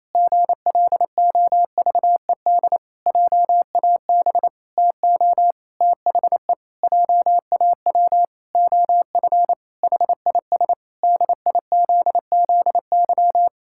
Morse code, heard as a keyed tone, about 27 wpm, 700 Hz. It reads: GLOVED JAB TO THE JAW OF HIS DIZZY
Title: Pangram Challenge v2 - 28wpm